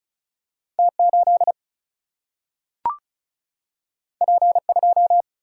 Morse code, heard as a keyed tone, about 36 wpm, 700 Hz. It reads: T8 P2
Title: Single-Letter Single-Number 35wpm